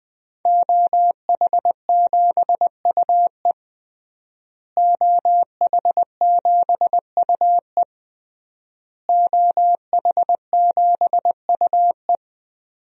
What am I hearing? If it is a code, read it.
Morse code, heard as a keyed tone, about 20 wpm, 700 Hz. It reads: OH7UE OH7UE OH7UE
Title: DXCC-100 Call Signs - 5 Characters - Repeated 3x - 20wpm